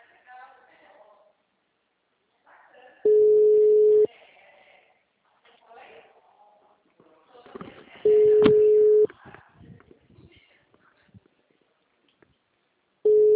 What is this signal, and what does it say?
Ringback tone (Europe, 425 Hz), with speech.